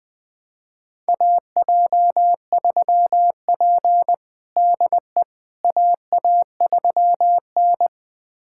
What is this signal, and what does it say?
Morse code, heard as a keyed tone, about 20 wpm, 700 Hz. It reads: AJ3P DE AA3N
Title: USA Callsign DE Callsign - 4 Characters - 20wpm